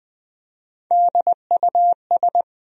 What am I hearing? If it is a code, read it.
Morse code, heard as a keyed tone, about 20 wpm, 700 Hz. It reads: DUS
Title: USA Call Sign Postfixes - 3 Letters 20wpm